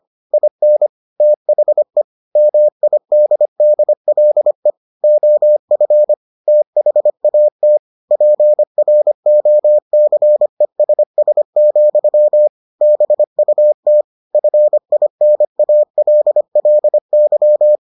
Morse code, heard as a keyed tone, about 25 wpm, 600 Hz. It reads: IN THE MIDDLE OF THAT PROCESS, BUT FINALLY